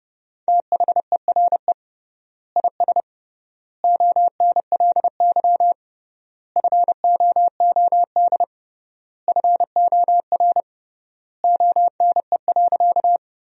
Morse code, heard as a keyed tone, about 32 wpm, 700 Hz. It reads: THERE IS ONLY FOOD FOR ONE.
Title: 3x Word-Spacing - Sentences from Top 300 Words - 30wpm